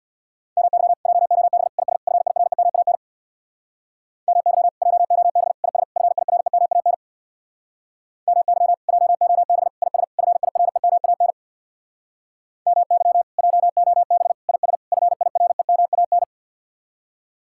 Morse code, heard as a keyed tone, about 59 wpm, 700 Hz. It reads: MY JOB IS FIREMAN MY JOB IS FIREMAN MY JOB IS FIREMAN MY JOB IS FIREMAN
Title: Speed-Racing Common QSO Phrases Encore - 50wpm